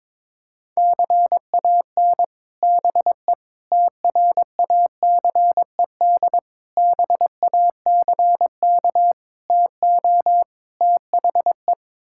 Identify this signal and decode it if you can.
Morse code, heard as a keyed tone, about 22 wpm, 700 Hz. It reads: CAN BE TRACED BACK TO THE
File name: practice / 22wpm / Sets of 6 Words - 22wpm